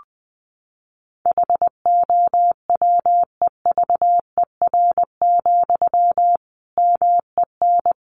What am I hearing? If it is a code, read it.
Morse code, heard as a keyed tone, about 20 wpm, 700 Hz. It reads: HOWEVER, MEN
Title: Sentences from Top 600 Words 20wpm